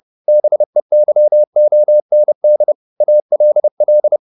Morse code, heard as a keyed tone, about 30 wpm, 600 Hz. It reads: BEYOND ALL
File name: the old man and the sea / the old man and the sea 1 30wpm